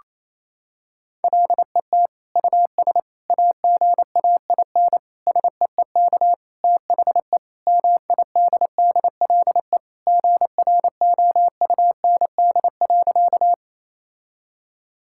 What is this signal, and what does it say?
Morse code, heard as a keyed tone, about 28 wpm, 700 Hz. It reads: LET US AGAIN SEEK THE MIDDLE GROUND.
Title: Sentences from Top 1600 Words 28wpm